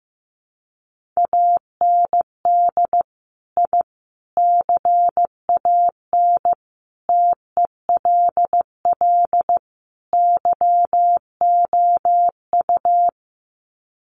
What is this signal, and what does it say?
Morse code, heard as a keyed tone, about 15 wpm, 700 Hz. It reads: AND I CAN TELL YOU
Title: Sets of 5 Words Encore2 15wpm